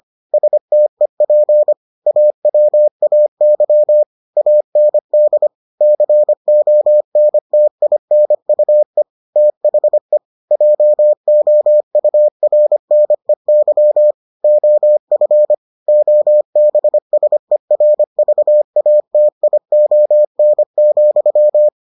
Morse code, heard as a keyed tone, about 25 wpm, 600 Hz. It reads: STEP AWAY AND CONTINUE THE JOURNEY OF OBSERVATION,